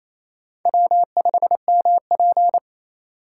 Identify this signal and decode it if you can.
Morse code, heard as a keyed tone, about 28 wpm, 700 Hz. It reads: W5MP
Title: USA Call Signs - 4 Characters - 3x Character Spacing - 28wpm